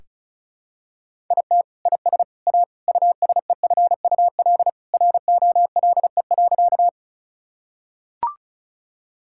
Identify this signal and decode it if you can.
Morse code, heard as a keyed tone, about 36 wpm, 700 Hz. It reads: IT IS A USEFUL ROLE.
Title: Sentences from Top 1000 Words - Review Part 6 - 35wpm